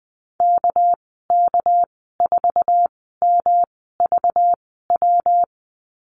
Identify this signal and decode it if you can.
Morse code, heard as a keyed tone, about 19 wpm, spaced as for 14 wpm, 700 Hz. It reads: KK4MVW
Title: 200 Lesson 32g - Letter k - Call Signs 20wpm